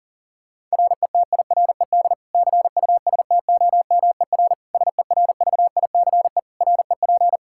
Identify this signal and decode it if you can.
Morse code, heard as a keyed tone, about 40 wpm, 700 Hz. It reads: RETIRED CUSTOMER SERVICE REP
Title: K1USN SST QSO Practice 40wpm